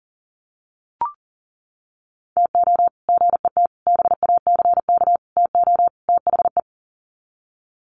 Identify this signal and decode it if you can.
Morse code, heard as a keyed tone, about 40 wpm, 700 Hz. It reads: TO GET BACK TO THE
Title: Sets of 5 Words Encore 40wpm